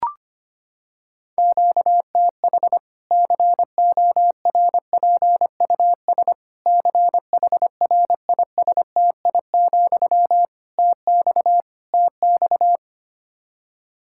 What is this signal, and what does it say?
Morse code, heard as a keyed tone, about 25 wpm, 700 Hz. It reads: QTH CORPUS CHRISTI, TX TX
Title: Practice QSO - Exchange 1 - Beginning - 25wpm